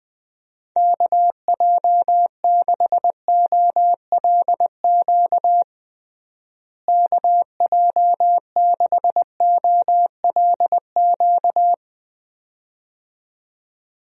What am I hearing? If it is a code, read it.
Morse code, heard as a keyed tone, about 20 wpm, 700 Hz. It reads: KJ6OLQ KJ6OLQ
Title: USA Call Signs - 6 Characters - Repeated 2x - 20wpm